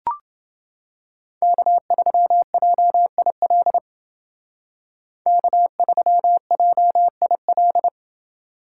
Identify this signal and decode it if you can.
Morse code, heard as a keyed tone, about 28 wpm, 700 Hz. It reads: K3JIL K3JIL